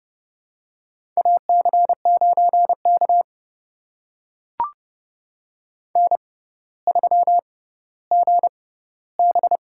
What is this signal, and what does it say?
Morse code, heard as a keyed tone, about 30 wpm, spaced as for 15 wpm, 700 Hz. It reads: AC9K N3GB
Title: USA Call Signs - 4 Characters - 6x Character Spacing - 30wpm